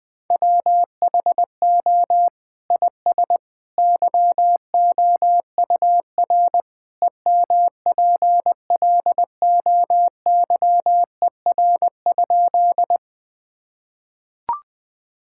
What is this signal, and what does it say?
Morse code, heard as a keyed tone, about 20 wpm, 700 Hz. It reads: WHO IS YOUR EMPLOYER?
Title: Sentences from Top 1600 Words 20wpm